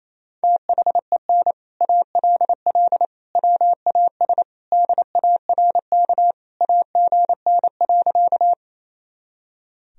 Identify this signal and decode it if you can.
Morse code, heard as a keyed tone, about 28 wpm, 700 Hz. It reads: THEN ALL WAS DARK AGN.
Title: Sentences from Top 500 Words and Abbreviations 28wpm